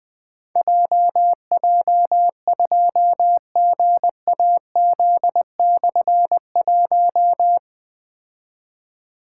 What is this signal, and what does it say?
Morse code, heard as a keyed tone, about 20 wpm, 700 Hz. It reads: JJ2GAZ/1
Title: Difficult Call Signs 20wpm